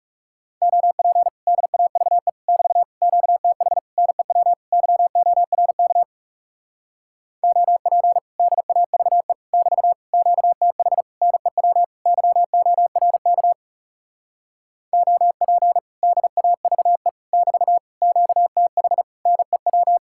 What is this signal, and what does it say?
Morse code, heard as a keyed tone, about 40 wpm, 700 Hz. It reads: OP DAVE <BT> QTH NEW YORK OP DAVE <BT> QTH NEW YORK OP DAVE <BT> QTH NEW